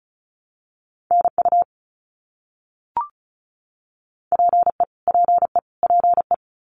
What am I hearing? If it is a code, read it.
Morse code, heard as a keyed tone, about 34 wpm, 700 Hz. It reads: NU PE PE PE
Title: Canadian Provinces and Territories - Repeated 3x - 35wpm